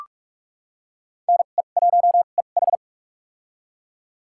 Morse code, heard as a keyed tone, about 46 wpm, spaced as for 32 wpm, 700 Hz. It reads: NE1EH
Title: USA Call Signs - 2x Character Spacing - 45wpm